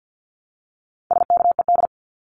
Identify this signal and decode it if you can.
Morse code, heard as a keyed tone, about 50 wpm, 700 Hz. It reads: SKED
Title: Abbreviations 50wpm